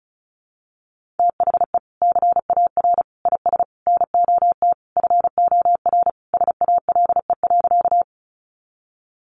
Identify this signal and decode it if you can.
Morse code, heard as a keyed tone, about 34 wpm, 700 Hz. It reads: THE CAR IS NOT FOR SALE.
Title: Sentences from Top 1400 Words 35wpm